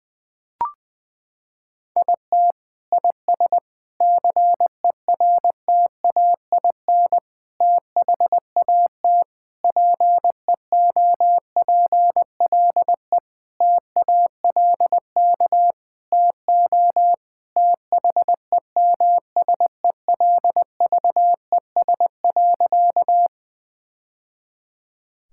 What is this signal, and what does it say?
Morse code, heard as a keyed tone, about 20 wpm, 700 Hz. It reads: IT IS CERTAIN THAT PEOPLE TALK TO THEMSELVES.
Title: Sentences from Top 1000 Words - Review Part 5 - 20wpm